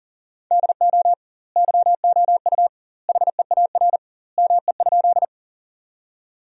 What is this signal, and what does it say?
Morse code, heard as a keyed tone, about 40 wpm, 700 Hz. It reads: DO YOU HEAR ME?
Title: 2x Word-Spacing - Sentences from Top 300 Words - 40wpm